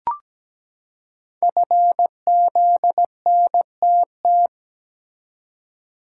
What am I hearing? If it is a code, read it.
Morse code, heard as a keyed tone, about 17 wpm, 700 Hz. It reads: FZNTT
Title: Code Groups - 5 Letters - 17wpm